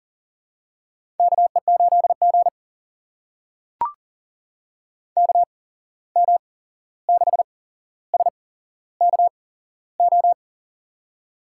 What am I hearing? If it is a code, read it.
Morse code, heard as a keyed tone, about 37 wpm, spaced as for 15 wpm, 700 Hz. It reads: KE8G KM6SKO
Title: USA Call Signs - 8x Character Spacing - 40wpm